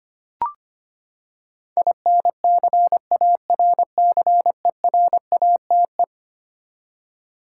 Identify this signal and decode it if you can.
Morse code, heard as a keyed tone, about 25 wpm, 700 Hz. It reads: INCARCERATE
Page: morse-code-ninja-practice-25wpm